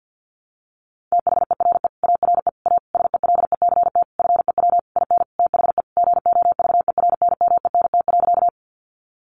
Morse code, heard as a keyed tone, about 48 wpm, 700 Hz. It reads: THERE ARE A SELECT FEW IN THE GOVERNMENT.
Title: Sentences from Top 1300 Words 50wpm